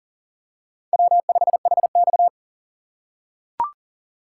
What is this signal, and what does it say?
Morse code, heard as a keyed tone, about 40 wpm, 700 Hz. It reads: W5HX
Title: USA Call Signs - 4 Characters - 3x Character Spacing - 40wpm